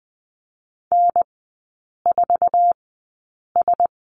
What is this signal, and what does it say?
Morse code, heard as a keyed tone, about 20 wpm, spaced as for 8 wpm, 700 Hz. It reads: N4S
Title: 074 Lesson 13e - Letter L - Three Characters 20wpm